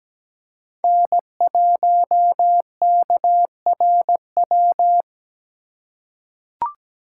Morse code, heard as a keyed tone, about 17 wpm, 700 Hz. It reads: N1KRW
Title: USA Call Signs - 3x Character Spacing - 17wpm